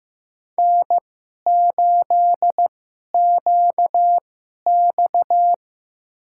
Morse code, heard as a keyed tone, about 15 wpm, spaced as for 11 wpm, 700 Hz. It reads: N8QX
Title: USA Call Signs - 4 Characters - 2x Character Spacing - 15wpm